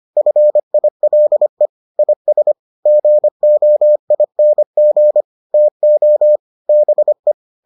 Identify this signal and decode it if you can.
Morse code, heard as a keyed tone, about 25 wpm, 600 Hz. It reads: FILE IS GOING TO BE